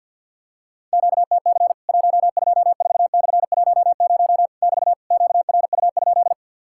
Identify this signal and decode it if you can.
Morse code, heard as a keyed tone, about 50 wpm, 700 Hz. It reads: QTC 124/10 <BT> QRU?
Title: Worked All Europe DX Contest - QTC Exchange - 50wpm